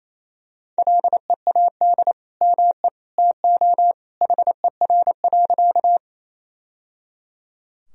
Morse code, heard as a keyed tone, about 28 wpm, 700 Hz. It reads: LEAD ME TO HER.
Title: Sentences from Top 1000 Words - Review Part 2 - 28wpm